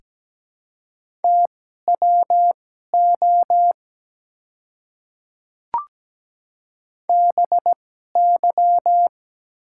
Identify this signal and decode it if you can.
Morse code, heard as a keyed tone, about 17 wpm, spaced as for 12 wpm, 690 Hz. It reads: TWO BY
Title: 2x Character Spacing - Top 100 Words - 17wpm